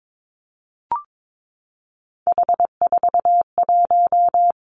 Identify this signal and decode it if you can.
Morse code, heard as a keyed tone, about 22 wpm, 700 Hz. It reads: H41
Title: Three Challenging Letter-Number Combinations 22wpm